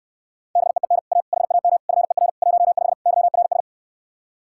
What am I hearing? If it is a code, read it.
Morse code, heard as a keyed tone, about 69 wpm, 700 Hz. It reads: BEEN A HAM FER 15 YRS